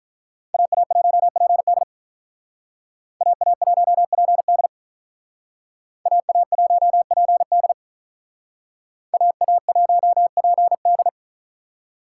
Morse code, heard as a keyed tone, about 42 wpm, 700 Hz. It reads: AA1PD AA1PD AA1PD AA1PD
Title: Speed Racing - USA 2x2 Call Signs 35wpm